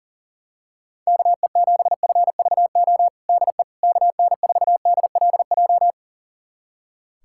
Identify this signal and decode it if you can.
Morse code, heard as a keyed tone, about 40 wpm, 690 Hz. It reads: KE7FVO DE KN4DLJ
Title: USA Callsign DE Callsign - 6 Characters - 40wpm